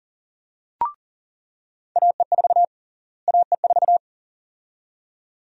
Morse code, heard as a keyed tone, about 40 wpm, 700 Hz. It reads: AE4 AE4
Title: USA Call Sign Prefixes - Repeated 2x - 40wpm